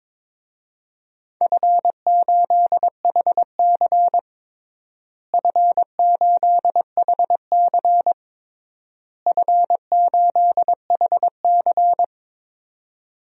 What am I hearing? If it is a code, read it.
Morse code, heard as a keyed tone, about 22 wpm, 700 Hz. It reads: F8HC F8HC F8HC
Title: DXCC-100 Call Signs - 4 Characters - Repeated 3x - 22wpm